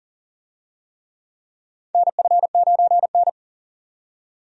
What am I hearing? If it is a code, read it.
Morse code, heard as a keyed tone, about 40 wpm, 700 Hz. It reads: NF9N
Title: USA Call Signs - 4 Characters - 40wpm